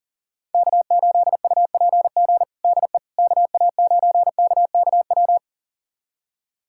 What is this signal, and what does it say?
Morse code, heard as a keyed tone, about 37 wpm, 700 Hz. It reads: K8UPG DE KA9KKW
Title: USA Callsign DE Callsign 40wpm